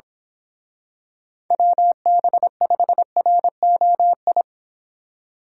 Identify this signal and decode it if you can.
Morse code, heard as a keyed tone, about 25 wpm, 700 Hz. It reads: WB5ROI
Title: Speed Racing - USA 2x3 Call Signs 22wpm